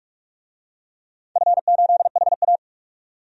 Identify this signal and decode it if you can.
Morse code, heard as a keyed tone, about 46 wpm, 700 Hz. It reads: F8HA
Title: DXCC-100 Call Signs - 4 Characters - 45wpm